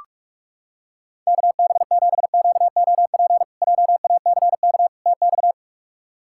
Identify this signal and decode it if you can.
Morse code, heard as a keyed tone, about 46 wpm, 700 Hz. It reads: KB7QOP JACK TX
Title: K1USN SST QSO Practice - 45wpm